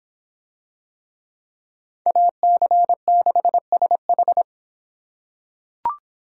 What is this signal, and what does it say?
Morse code, heard as a keyed tone, about 26 wpm, 700 Hz. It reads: AC6SH